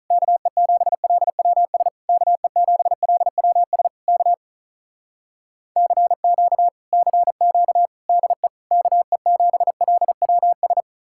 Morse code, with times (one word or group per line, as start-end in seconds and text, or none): 0.1-1.9 KE7LWS
2.1-3.9 KE7LWS
4.1-4.3 K
5.8-6.7 CQ
6.9-7.9 CQ
8.1-8.5 DE
8.7-10.8 KE7LWS